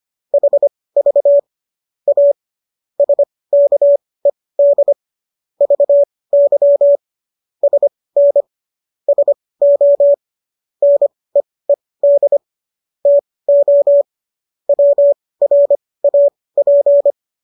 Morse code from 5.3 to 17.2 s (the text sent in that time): VY SN SO NEED TO WRAP